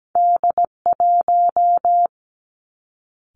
Morse code, heard as a keyed tone, about 17 wpm, 700 Hz. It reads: D1
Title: Two Challenging Letter-Number Combinations 17wpm